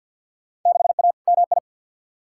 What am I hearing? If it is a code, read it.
Morse code, heard as a keyed tone, about 52 wpm, 700 Hz. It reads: 6A MI
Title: Field Day Exchanges - 50wpm